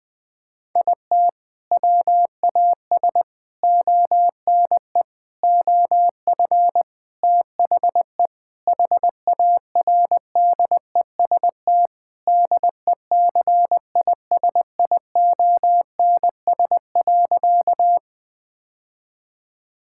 Morse code, heard as a keyed tone, about 21 wpm, 700 Hz. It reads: IT WAS ONE OF THE HARDEST DECISIONS.
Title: Sentences with 5-Word Phrases - 20wpm